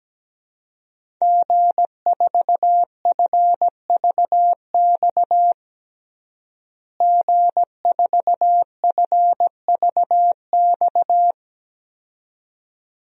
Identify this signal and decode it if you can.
Morse code, heard as a keyed tone, about 17 wpm, 700 Hz. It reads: G4FVX G4FVX
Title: DXCC-100 Call Signs - 5 Characters - Repeated 2x - 17wpm